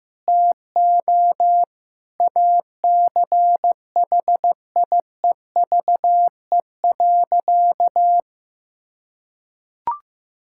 Morse code, heard as a keyed tone, about 16 wpm, 700 Hz. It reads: TO ACHIEVE.
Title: Sentences from Top 1000 Words - Review Part 6 - 15wpm